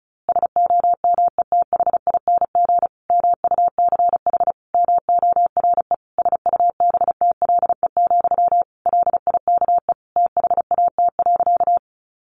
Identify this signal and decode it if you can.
Morse code, heard as a keyed tone, about 34 wpm, 700 Hz. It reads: SOMETHING MUCH MORE SUBTLE, LIKE THAT.
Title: A Sound Of Thunder 35wpm